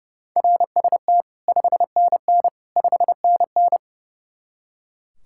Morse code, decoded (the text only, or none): RST 5NN 5NN